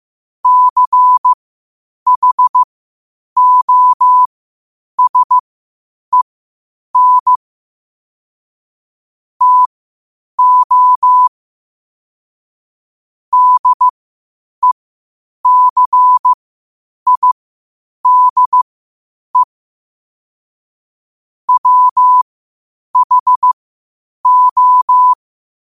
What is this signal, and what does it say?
Morse code, heard as a keyed tone, about 15 wpm, spaced as for 8 wpm, 1000 Hz. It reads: CHOSEN TO DECIDE WHO
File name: qotdfiles / 8wpm